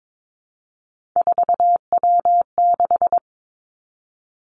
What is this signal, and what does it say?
Morse code, heard as a keyed tone, about 22 wpm, 700 Hz. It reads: 4W6
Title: Three Challenging Letter-Number Combinations 22wpm